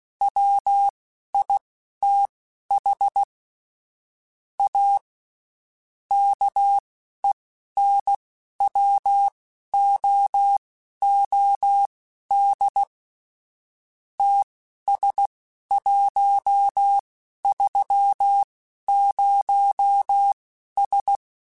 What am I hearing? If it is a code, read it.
Morse code, heard as a keyed tone, about 16 wpm, spaced as for 11 wpm, 800 Hz. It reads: WITH A KENWOOD TS130S